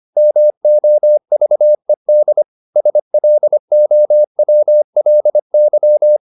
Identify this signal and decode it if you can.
Morse code, heard as a keyed tone, about 25 wpm, 600 Hz. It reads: MOVED SLOWLY
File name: the old man and the sea part 1 25wpm